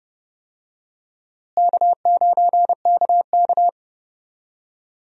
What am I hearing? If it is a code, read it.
Morse code, heard as a keyed tone, about 32 wpm, 700 Hz. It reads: K9KK
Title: USA Call Signs - 4 Characters - Repeated 3x - 30wpm